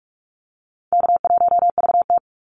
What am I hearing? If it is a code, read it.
Morse code, heard as a keyed tone, about 44 wpm, 700 Hz. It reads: K1VT
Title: Easy Call Signs Repeated 3x - 45wpm